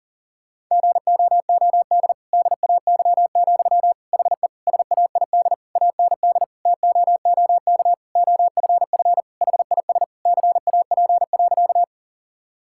Morse code, heard as a keyed tone, about 40 wpm, 700 Hz. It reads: GOOD DAY, HE SAID AND TOOK OFF HIS CAP.
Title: Sentences from Top 1600 Words 40wpm